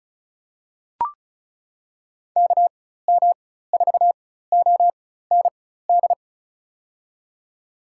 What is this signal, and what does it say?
Morse code, heard as a keyed tone, about 34 wpm, spaced as for 16 wpm, 700 Hz. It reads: KM4OND
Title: USA Call Signs - 6 Characters - 4x Character Spacing - 35wpm